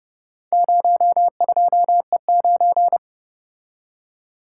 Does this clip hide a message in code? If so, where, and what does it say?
Yes: Morse code from 0.5 to 3.0 s, heard as a keyed tone, about 30 wpm, 700 Hz. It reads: 02E9